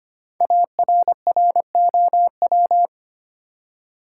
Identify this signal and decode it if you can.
Morse code, heard as a keyed tone, about 25 wpm, 700 Hz. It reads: ARROW
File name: 5 Letter Words - First 3 Letters Given - 25wpm